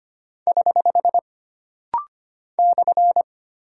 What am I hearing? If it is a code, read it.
Morse code, heard as a keyed tone, about 25 wpm, 700 Hz. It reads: <HH> /
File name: Single Punctuation-Prosign - Rapid-Fire - 25wpm